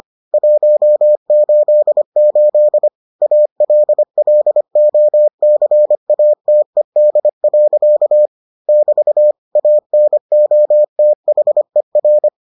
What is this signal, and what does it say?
Morse code, heard as a keyed tone, about 25 wpm, 600 Hz. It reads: 188 ALLOCATED. <BT> ANOTHER